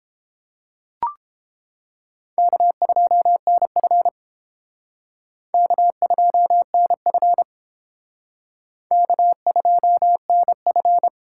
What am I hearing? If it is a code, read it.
Morse code, heard as a keyed tone, about 31 wpm, 700 Hz. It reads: K2NF K2NF K2NF